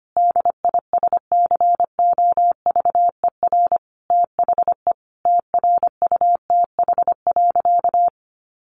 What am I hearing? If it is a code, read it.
Morse code, heard as a keyed tone, about 25 wpm, 700 Hz. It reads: DISCOVER THE TRUTH.
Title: Sentences from Top 1100 Words 25wpm